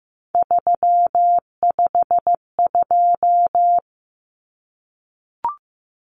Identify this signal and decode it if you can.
Morse code, heard as a keyed tone, about 15 wpm, 700 Hz. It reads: 352
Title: Numbers - 3-Digit 15wpm